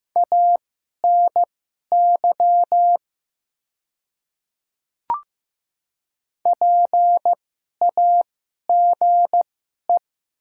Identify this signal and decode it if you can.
Morse code, heard as a keyed tone, about 15 wpm, spaced as for 11 wpm, 700 Hz. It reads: ANY PAGE
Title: 2x Character Spacing - Top 300 Words - 15wpm